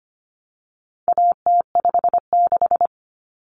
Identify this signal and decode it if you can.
Morse code, heard as a keyed tone, about 25 wpm, 700 Hz. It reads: AT56
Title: Common Cut Numbers - 4-Digit 25wpm